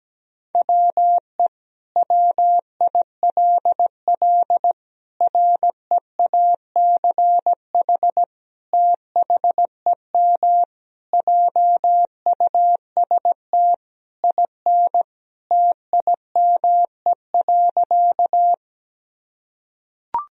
Morse code, heard as a keyed tone, about 17 wpm, 700 Hz. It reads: WE WILL REACH THEM JUST IN TIME.
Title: Sentences from Top 1000 Words - Review Part 2 - 17wpm